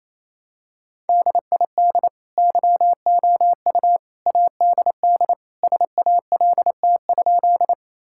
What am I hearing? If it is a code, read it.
Morse code, heard as a keyed tone, about 28 wpm, 700 Hz. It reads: DID YOU ADD SALT?
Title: Sentences from Top 1000 Words - Review Part 5 - 28wpm